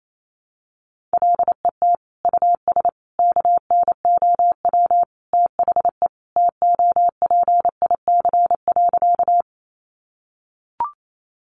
Morse code, heard as a keyed tone, about 28 wpm, 700 Hz. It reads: LET US KNOW THE TOPIC.
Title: Sentences from Top 1200 Words 28wpm